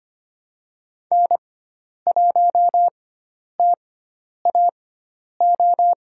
Morse code, heard as a keyed tone, about 25 wpm, spaced as for 10 wpm, 700 Hz. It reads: N1TAO
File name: Lesson 16g - Letter c - Call Signs 25wpm